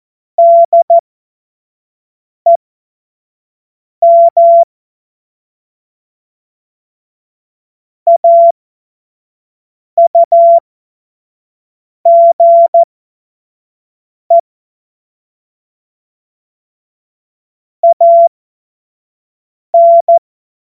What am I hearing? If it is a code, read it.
Morse code, heard as a keyed tone, about 14 wpm, spaced as for 5 wpm, 680 Hz. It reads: DEM AUGE AN
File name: Morsekurs / Witze / Witz 02 Januar 2024 25 F70